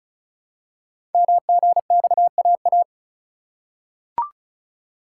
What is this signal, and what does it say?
Morse code, heard as a keyed tone, about 36 wpm, 700 Hz. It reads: MGXAA
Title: Code Groups - 5 Letters - 35wpm